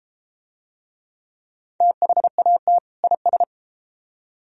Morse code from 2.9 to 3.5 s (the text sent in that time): IS